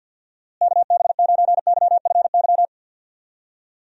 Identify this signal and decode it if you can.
Morse code, heard as a keyed tone, about 52 wpm, 700 Hz. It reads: KB9YFY
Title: USA Call Signs - 6 Characters - Repeated 3x - 50wpm